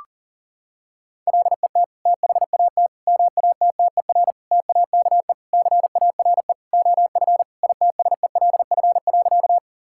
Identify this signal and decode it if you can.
Morse code, heard as a keyed tone, about 40 wpm, 700 Hz. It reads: LET THAT MATTER TAKE CARE OF ITSELF.
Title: Sentences from Top 1000 Words - Review Part 5 - 40wpm